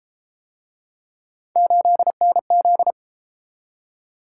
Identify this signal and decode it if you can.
Morse code, heard as a keyed tone, about 32 wpm, 700 Hz. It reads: 8NZ